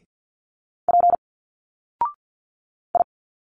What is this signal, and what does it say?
Morse code, heard as a keyed tone, about 51 wpm, 700 Hz. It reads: P I
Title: Single Letters 50wpm